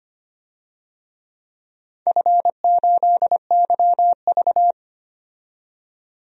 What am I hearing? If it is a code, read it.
Morse code, heard as a keyed tone, about 25 wpm, 700 Hz. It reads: F8YV
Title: DXCC-100 Call Signs - 4 Characters - 25wpm